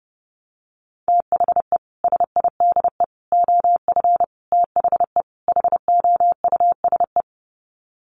Morse code, heard as a keyed tone, about 29 wpm, 700 Hz. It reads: THE SIDE OF THE HOUSE
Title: Sets of 5 Words Encore2 30wpm